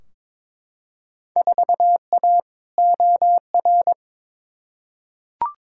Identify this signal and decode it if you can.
Morse code, heard as a keyed tone, about 22 wpm, 700 Hz. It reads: 4A OR
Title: Field Day Exchanges - 22wpm